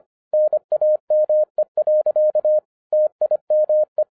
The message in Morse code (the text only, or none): NAME. TIME